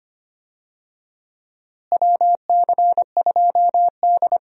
Morse code, heard as a keyed tone, about 25 wpm, 700 Hz. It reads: WC2D
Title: USA Call Signs - 4 Characters - Repeated 3x - 25wpm